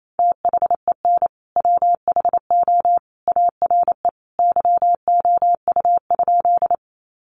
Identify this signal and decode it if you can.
Morse code, heard as a keyed tone, about 28 wpm, 700 Hz. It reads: THEN WHO ARE YOU?
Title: Sentences from Top 100 Words 28wpm